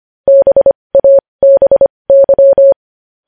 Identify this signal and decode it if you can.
Morse code, heard as a keyed tone, about 25 wpm, spaced as for 20 wpm, 550 Hz. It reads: BABY
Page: news headlines in morse code at 20 wpm